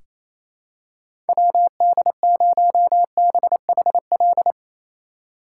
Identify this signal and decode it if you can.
Morse code, heard as a keyed tone, about 27 wpm, 700 Hz. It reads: WD0BHL
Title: USA Call Signs - 6 Characters - 6x Character Spacing - 28wpm